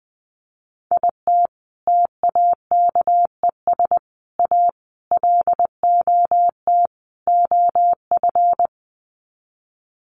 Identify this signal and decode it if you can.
Morse code, heard as a keyed tone, about 20 wpm, 700 Hz. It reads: IT TAKES A LOT OF
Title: Sets of 5 Words Encore 20wpm